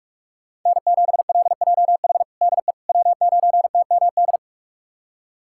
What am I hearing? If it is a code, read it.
Morse code, heard as a keyed tone, about 41 wpm, 700 Hz. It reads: N7LJH DE W9TMD
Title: USA Callsign DE Callsign - 5 Characters - 45wpm